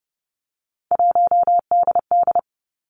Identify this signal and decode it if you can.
Morse code, heard as a keyed tone, about 30 wpm, 700 Hz. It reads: 1DD
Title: Three Challenging Letter-Number Combinations 30wpm